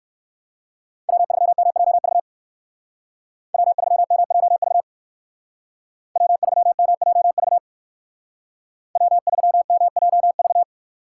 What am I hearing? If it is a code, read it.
Morse code, heard as a keyed tone, about 53 wpm, 700 Hz. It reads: W3MJV W3MJV W3MJV W3MJV
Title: Speed Racing - USA 1x3 Call Signs 45wpm